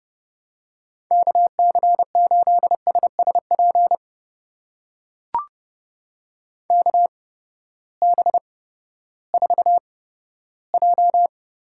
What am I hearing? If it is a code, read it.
Morse code, heard as a keyed tone, about 30 wpm, spaced as for 15 wpm, 700 Hz. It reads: KC8SSP KB4J
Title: USA Call Signs - 6 Characters - 8x Character Spacing - 30wpm